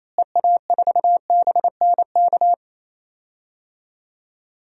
Morse code, heard as a keyed tone, about 28 wpm, 700 Hz. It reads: EA4BNK
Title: DXCC-100 Call Signs - 6 Characters - Repeated 2x - 28wpm